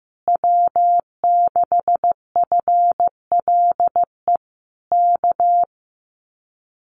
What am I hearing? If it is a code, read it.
Morse code, heard as a keyed tone, about 15 wpm, 700 Hz. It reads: W6FLE K
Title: Common QSO Phrases Encore 15wpm